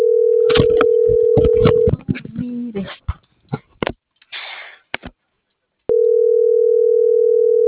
Ringback tone (North America), 440 + 480 Hz.